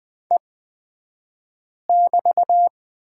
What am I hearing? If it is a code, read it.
Morse code, heard as a keyed tone, about 20 wpm, spaced as for 5 wpm, 690 Hz. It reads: E<BT>
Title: 228 Lesson 37d - Letter x - Two Characters 20wpm